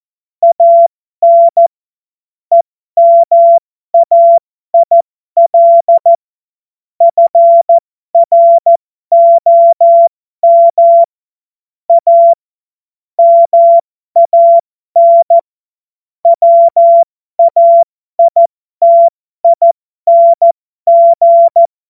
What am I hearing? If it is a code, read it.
Morse code, heard as a keyed tone, about 14 wpm, spaced as for 12 wpm, 680 Hz. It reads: AN EMAIL FROM A MAN WAITING
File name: Morsekurs / Witze